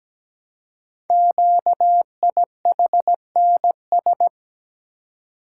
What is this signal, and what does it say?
Morse code, heard as a keyed tone, about 17 wpm, 700 Hz. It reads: QIHNS
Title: Code Groups - 5 Letters - 17wpm